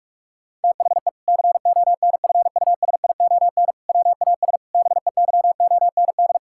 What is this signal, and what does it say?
Morse code, heard as a keyed tone, about 41 wpm, 700 Hz. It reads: THE CONFUSION WAS BEYOND
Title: Sentences from Top 1600 Words 45wpm